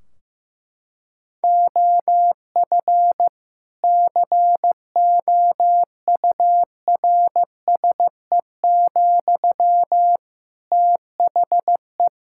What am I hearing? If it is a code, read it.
Morse code, heard as a keyed tone, about 15 wpm, 700 Hz. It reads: OF COURSE, THE
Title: Sentences from Top 1600 Words 15wpm